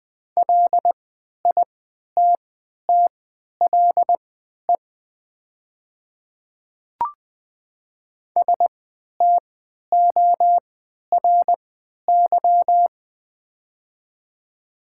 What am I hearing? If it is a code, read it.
Morse code, heard as a keyed tone, about 20 wpm, spaced as for 11 wpm, 700 Hz. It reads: LITTLE STORY
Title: 3x Character Spacing - Top 300 Words - 20wpm